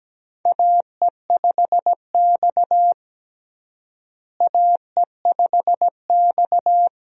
Morse code, heard as a keyed tone, about 17 wpm, 700 Hz. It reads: AE5X AE5X
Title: USA Call Signs - 4 Characters - Repeated 3x - 17wpm